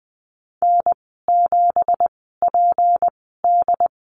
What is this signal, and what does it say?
Morse code, heard as a keyed tone, about 20 wpm, spaced as for 14 wpm, 700 Hz. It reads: N7PD
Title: 213 Lesson 34g - Number 8 - Call Signs 20wpm